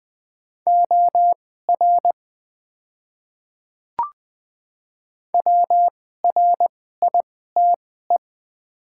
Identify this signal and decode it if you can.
Morse code, heard as a keyed tone, about 20 wpm, spaced as for 15 wpm, 700 Hz. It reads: OR WRITE
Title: 2x Character Spacing - Top 100 Words - 20wpm